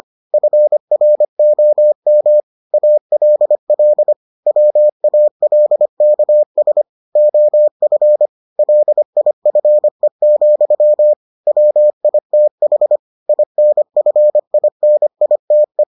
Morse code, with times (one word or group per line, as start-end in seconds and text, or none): 0.3-2.4 FROM
2.7-4.1 ALL
4.5-6.8 WALKS
7.2-8.3 OF
8.6-11.1 LIFE,
11.5-13.0 WITH
13.3-15.8 INFINITE